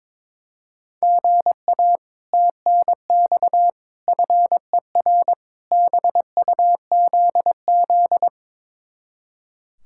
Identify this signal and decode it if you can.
Morse code, heard as a keyed tone, about 22 wpm, 700 Hz. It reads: GA TNX FER BUZZ
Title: Speed-Racing Common QSO Phrases - 22wpm